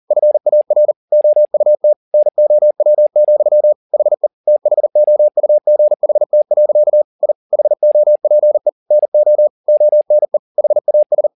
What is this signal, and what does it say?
Morse code, heard as a keyed tone, about 40 wpm, 600 Hz. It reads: FAR OUT NOW, HE THOUGHT. I HOPE NO ONE HAS
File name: the old man and the sea 2 40wpm